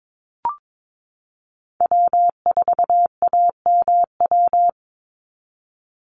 Morse code, heard as a keyed tone, about 22 wpm, 700 Hz. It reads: W4AMW
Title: NA SOTA Chaser Callsigns 22wpm